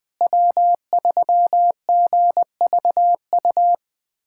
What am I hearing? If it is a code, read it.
Morse code, heard as a keyed tone, about 20 wpm, 700 Hz. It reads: W3GVU
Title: USA Callsign DE Callsign - 5 Characters - 20wpm